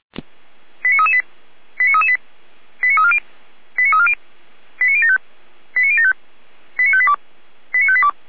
Five-tone selective call (ZVEI), heard as quick runs of five tones, tones of about 70 ms. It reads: 89338 89338 89340 89340 89075 89075 89582 89582